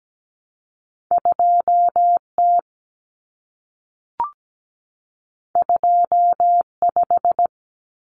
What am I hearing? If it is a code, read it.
Morse code, heard as a keyed tone, about 17 wpm, 700 Hz. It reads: 2T 25
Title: Common Cut Numbers - 2-Digit 17wpm